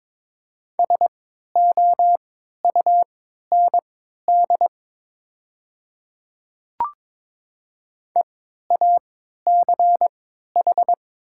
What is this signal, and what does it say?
Morse code, heard as a keyed tone, about 22 wpm, spaced as for 12 wpm, 700 Hz. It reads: SOUND EACH
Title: 3x Character Spacing - Top 100 Words - 22wpm